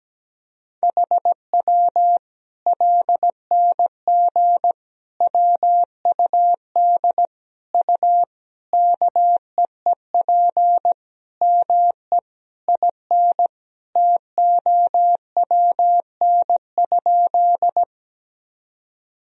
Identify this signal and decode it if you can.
Morse code, heard as a keyed tone, about 17 wpm, 700 Hz. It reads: HW LNG WUD U KEEP ME IN TOWN?
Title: Sentences from Top 500 Words and Abbreviations 17wpm